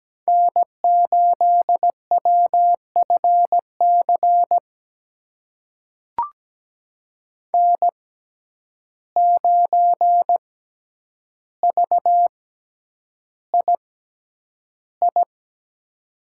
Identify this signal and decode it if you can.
Morse code, heard as a keyed tone, about 17 wpm, spaced as for 9 wpm, 700 Hz. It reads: N8WFC N9VII
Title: USA Call Signs - 5 Characters - 6x Character Spacing - 17wpm